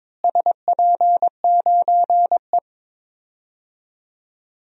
Morse code, heard as a keyed tone, about 22 wpm, 700 Hz. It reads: SP9E
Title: DXCC-100 Call Signs - 4 Characters - Repeated 3x - 22wpm